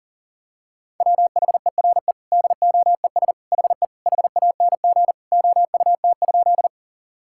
Morse code, heard as a keyed tone, about 40 wpm, 700 Hz. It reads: WHERE DOES HE HANG OUT?
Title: Sentences from Top 1600 Words 40wpm